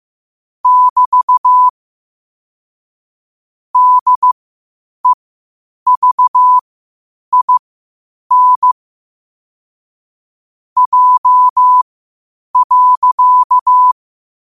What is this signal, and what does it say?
Morse code, heard as a keyed tone, about 15 wpm, spaced as for 8 wpm, 1000 Hz. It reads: <BT> DEVIN J.
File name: qotdfiles / 8wpm